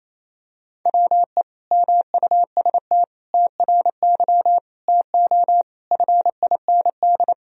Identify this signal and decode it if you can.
Morse code, heard as a keyed tone, about 28 wpm, 700 Hz. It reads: WE MUST TRY TO FIND
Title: Sentences from Top 400 Words and Abbreviations 28wpm